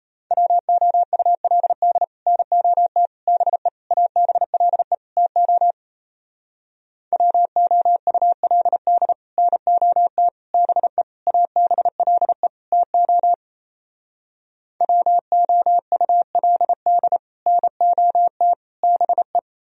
Morse code, heard as a keyed tone, about 37 wpm, 700 Hz. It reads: WOULD NOT BE ABLE TO WOULD NOT BE ABLE TO WOULD NOT BE